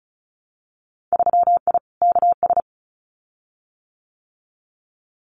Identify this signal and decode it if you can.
Morse code, heard as a keyed tone, about 34 wpm, 700 Hz. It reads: 3I KS
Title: Winter Field Day Exchanges 35wpm